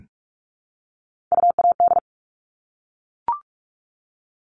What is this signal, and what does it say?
Morse code, heard as a keyed tone, about 44 wpm, 700 Hz. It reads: UAD
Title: All Cut Numbers - 3-Digit 45wpm